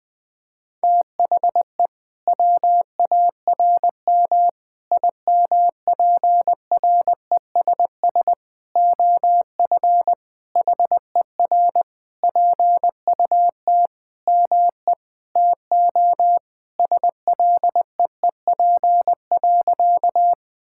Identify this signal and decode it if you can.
Morse code, heard as a keyed tone, about 20 wpm, 700 Hz. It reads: THE WARM IMPRESS OF HER PUT ME TO SLEEP.
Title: Sentences from Top 1600 Words 20wpm